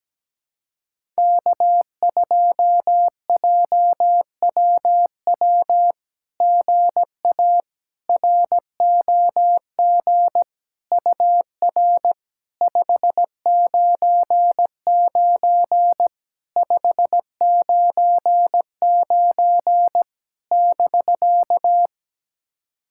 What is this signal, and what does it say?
Morse code, heard as a keyed tone, about 17 wpm, 700 Hz. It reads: K2JWW GA ROG UR 599 599 <BK>
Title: SOTA QSO Practice 17wpm